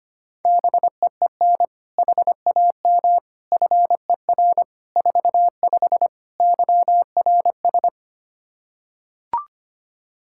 Morse code, heard as a keyed tone, about 25 wpm, 700 Hz. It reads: BEEN HAM FER 45 YRS
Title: K1USN SST QSO Practice 25wpm